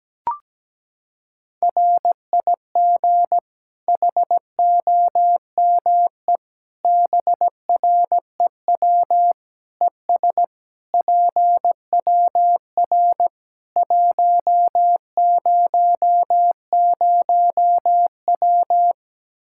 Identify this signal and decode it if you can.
Morse code, heard as a keyed tone, about 17 wpm, 700 Hz. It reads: RIG HOME BREW ES PWR 100W
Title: Practice QSO - Exchange 2 - Info - 17wpm